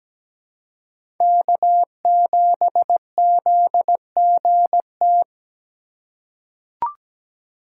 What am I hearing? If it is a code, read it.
Morse code, heard as a keyed tone, about 17 wpm, 700 Hz. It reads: K7ZGT
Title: USA Call Signs - 5 Characters - 2x Character Spacing - 17wpm